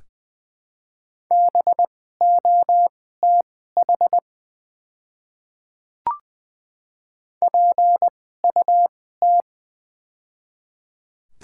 Morse code, heard as a keyed tone, about 20 wpm, spaced as for 14 wpm, 700 Hz. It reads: BOTH PUT
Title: 2x Character Spacing - Top 300 Words - 20wpm